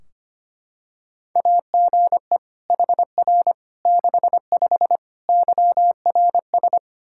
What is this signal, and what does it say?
Morse code, heard as a keyed tone, about 25 wpm, 700 Hz. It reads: AGE HR 65 YRS
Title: Practice QSO - Exchange 3 - More Info - 25wpm